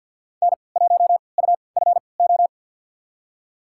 Morse code, heard as a keyed tone, about 52 wpm, spaced as for 29 wpm, 700 Hz. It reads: N1UFO
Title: USA Call Signs - 5 Characters - 3x Character Spacing - 50wpm